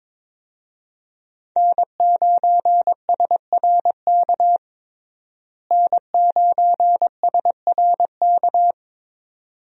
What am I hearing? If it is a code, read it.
Morse code, heard as a keyed tone, about 22 wpm, 700 Hz. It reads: N9SRK N9SRK
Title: USA Call Signs - 5 Characters - Repeated 2x - 22wpm